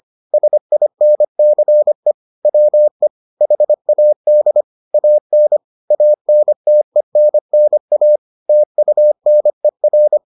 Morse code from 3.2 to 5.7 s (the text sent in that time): HAD AN